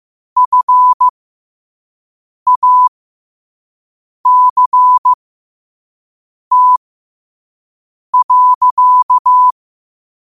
Morse code, heard as a keyed tone, about 15 wpm, spaced as for 5 wpm, 1000 Hz. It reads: FACT.
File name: qotdfiles / 5wpm